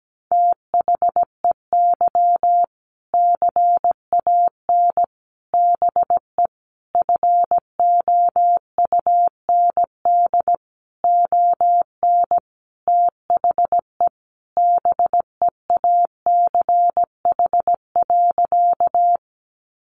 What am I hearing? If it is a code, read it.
Morse code, heard as a keyed tone, about 17 wpm, 700 Hz. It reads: THEY CAN BE FOUND ON THE BEACH.
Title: Sentences from Top 1500 Words 17wpm